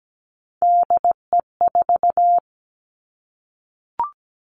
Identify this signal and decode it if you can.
Morse code, heard as a keyed tone, about 17 wpm, 700 Hz. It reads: DE4
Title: Three Challenging Letter-Number Combinations 17wpm